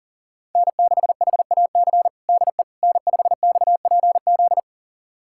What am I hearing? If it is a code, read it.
Morse code, heard as a keyed tone, about 40 wpm, 700 Hz. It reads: N6HAC DE N5XPZ
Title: USA Callsign DE Callsign - 5 Characters - 40wpm